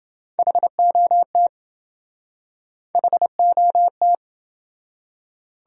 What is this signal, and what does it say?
Morse code, heard as a keyed tone, about 28 wpm, 700 Hz. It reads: HOT HOT